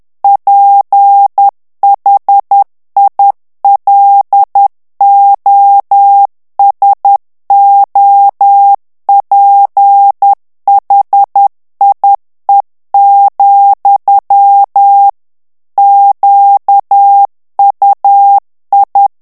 Morse code, heard as a keyed tone, about 10 wpm, 800 Hz. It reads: PHILOSOPHIE, QUI
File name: morse mp3 / FRANCAIS TEXTE CLAIR 2 VITESSE 10 MOTS MINUTES